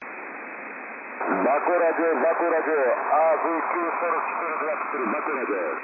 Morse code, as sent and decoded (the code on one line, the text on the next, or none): none